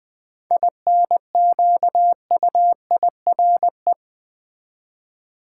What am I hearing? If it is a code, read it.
Morse code, heard as a keyed tone, about 20 wpm, 700 Hz. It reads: INQUIRE
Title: 7 Letter Words - First 3 Letters Given - 20wpm